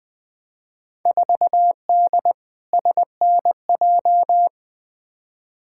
Morse code, heard as a keyed tone, about 20 wpm, 700 Hz. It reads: 4D SNJ
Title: Field Day Exchanges Encore - 20wpm